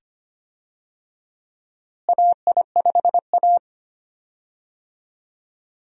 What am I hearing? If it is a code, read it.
Morse code, heard as a keyed tone, about 25 wpm, 700 Hz. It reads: AI5A
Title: USA Call Signs - 4 Characters - Repeated 2x - 25wpm